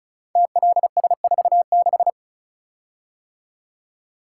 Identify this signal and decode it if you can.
Morse code, heard as a keyed tone, about 36 wpm, 700 Hz. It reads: TLS46
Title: Code Groups - 5 Letter:Numbers - 35wpm